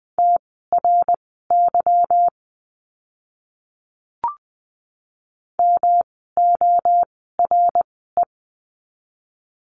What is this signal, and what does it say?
Morse code, heard as a keyed tone, about 20 wpm, spaced as for 15 wpm, 700 Hz. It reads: TRY MORE
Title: 160 Lesson 26f - Letter g - Words 20wpm